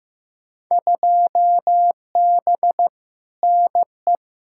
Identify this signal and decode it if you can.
Morse code, heard as a keyed tone, about 15 wpm, 700 Hz. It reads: 2B NE
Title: Field Day Exchanges Encore - 15wpm